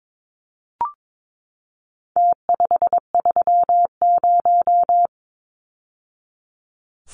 Morse code, heard as a keyed tone, about 22 wpm, 700 Hz. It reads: T530
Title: Numbers - Contest Style - 4-Digit 22wpm